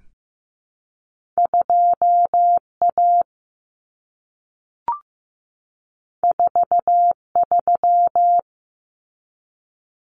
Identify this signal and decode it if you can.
Morse code, heard as a keyed tone, about 15 wpm, 700 Hz. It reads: 2A 43
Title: Common Cut Numbers - 2-Digit 15wpm